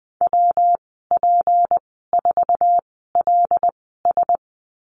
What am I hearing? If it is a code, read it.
Morse code, heard as a keyed tone, about 20 wpm, spaced as for 14 wpm, 700 Hz. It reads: WP4LS
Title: 200 Lesson 32g - Letter k - Call Signs 20wpm